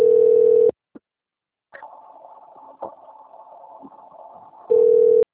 Ringback tone (North America), 440 + 480 Hz.